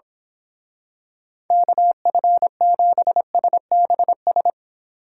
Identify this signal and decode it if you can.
Morse code, heard as a keyed tone, about 26 wpm, 700 Hz. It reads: KF7SBS
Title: Speed Racing - USA 2x3 Call Signs 22wpm